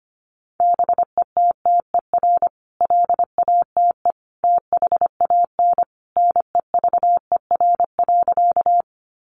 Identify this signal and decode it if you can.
Morse code, heard as a keyed tone, about 25 wpm, 700 Hz. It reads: BETTER LATE THAN NEVER.
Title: Sentences from Top 400 Words 25wpm